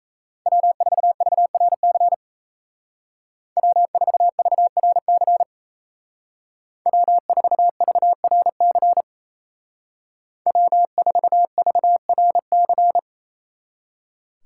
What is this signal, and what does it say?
Morse code, heard as a keyed tone, about 33 wpm, 700 Hz. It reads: W4VRC W4VRC W4VRC W4VRC
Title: Speed Racing - USA 1x3 Call Signs 28wpm